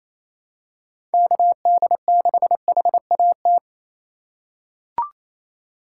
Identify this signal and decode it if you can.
Morse code, heard as a keyed tone, about 28 wpm, 700 Hz. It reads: KD6HAT
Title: USA Call Signs - 6 Characters - 3x Character Spacing - 28wpm